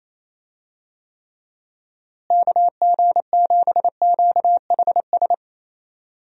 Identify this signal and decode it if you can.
Morse code, heard as a keyed tone, about 27 wpm, 700 Hz. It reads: KG7QHS
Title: USA Call Signs - 6 Characters - Repeated 2x - 28wpm